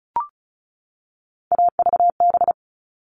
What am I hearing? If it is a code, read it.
Morse code, heard as a keyed tone, about 34 wpm, 700 Hz. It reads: AVB
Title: All Cut Numbers - 3-Digit 35wpm